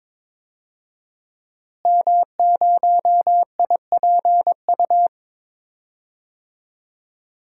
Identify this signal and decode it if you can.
Morse code, heard as a keyed tone, about 22 wpm, 700 Hz. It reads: M0IPU
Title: DXCC-100 Call Signs - 5 Characters - Repeated 2x - 22wpm